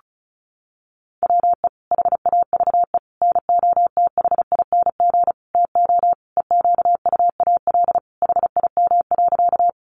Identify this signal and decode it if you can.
Morse code, heard as a keyed tone, about 34 wpm, 700 Hz. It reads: WE HAVE NOTHING TO EQUAL HIM.
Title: Sentences from Top 1400 Words 35wpm